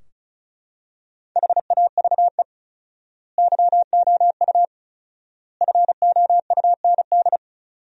Morse code, heard as a keyed tone, about 36 wpm, 700 Hz. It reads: HAVE YOU FOUND
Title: 4x Word-Spacing - Sentences from Top 300 Words - 35wpm